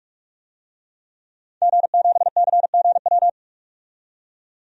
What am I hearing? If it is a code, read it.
Morse code, heard as a keyed tone, about 41 wpm, 700 Hz. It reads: G7CGW
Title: DXCC-100 Call Signs - 5 Characters - 45wpm